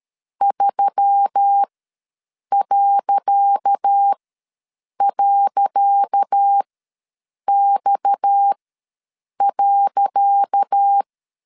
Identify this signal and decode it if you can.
Morse code, heard as a keyed tone, about 13 wpm, spaced as for 7 wpm, 800 Hz. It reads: 3..X.
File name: LEZIONI / 10 SIGNS / 10C DOT ws 12wpm 3548